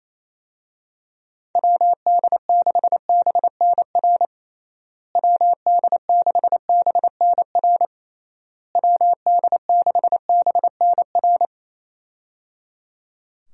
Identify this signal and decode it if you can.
Morse code, heard as a keyed tone, about 28 wpm, 700 Hz. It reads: WD6BNR WD6BNR WD6BNR
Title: USA Call Signs - 6 Characters - Repeated 3x - 28wpm